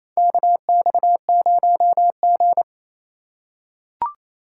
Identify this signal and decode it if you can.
Morse code, heard as a keyed tone, about 27 wpm, 700 Hz. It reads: KX0G
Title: USA Call Signs - 4 Characters - 4x Character Spacing - 28wpm